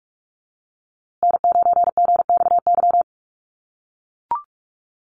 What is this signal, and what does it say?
Morse code, heard as a keyed tone, about 44 wpm, 700 Hz. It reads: N9GXY
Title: USA Call Signs Repeated 3x 45wpm